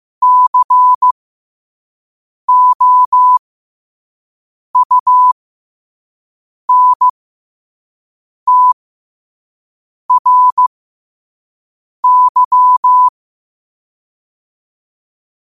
Morse code, heard as a keyed tone, about 15 wpm, spaced as for 5 wpm, 1000 Hz. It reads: COUNTRY
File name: qotdfiles / 5wpm